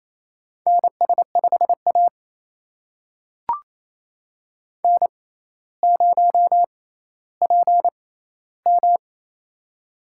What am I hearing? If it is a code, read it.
Morse code, heard as a keyed tone, about 27 wpm, spaced as for 14 wpm, 700 Hz. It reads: NS5A N0PM
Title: USA Call Signs - 4 Characters - 6x Character Spacing - 28wpm